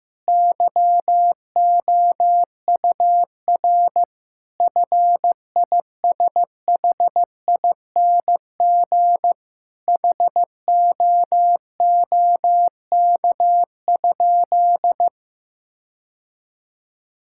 Morse code, heard as a keyed tone, about 16 wpm, 700 Hz. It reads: YOUR FISHING HOOK?
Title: Sentences from Top 1600 Words 15wpm